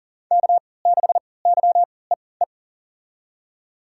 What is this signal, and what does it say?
Morse code, heard as a keyed tone, about 40 wpm, spaced as for 23 wpm, 700 Hz. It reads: K6YEE
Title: USA Call Signs - 5 Characters - 3x Character Spacing - 40wpm